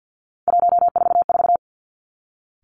Morse code, heard as a keyed tone, about 50 wpm, 700 Hz. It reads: 144
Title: Three Challenging Letter-Number Combinations 50wpm